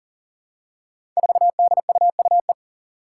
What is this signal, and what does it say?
Morse code, heard as a keyed tone, about 40 wpm, 700 Hz. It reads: 4DUUE
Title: Five Challenging Letter-Number Combinations 40wpm